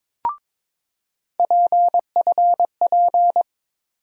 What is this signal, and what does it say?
Morse code, heard as a keyed tone, about 22 wpm, 700 Hz. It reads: PFP
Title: USA Call Sign Postfixes - 3 Letters 22wpm